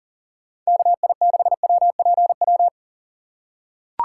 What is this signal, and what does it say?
Morse code, heard as a keyed tone, about 40 wpm, 700 Hz. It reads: KI6WPW
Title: USA Call Signs - 6 Characters - 3x Character Spacing - 40wpm